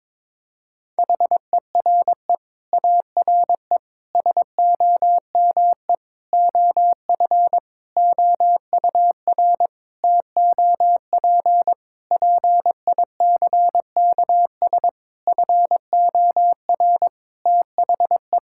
Morse code, heard as a keyed tone, about 22 wpm, 700 Hz. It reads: HERE ARE SOME OF OUR TOP PICKS FOR THE
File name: Sentences with 9-Word Phrases - 22wpm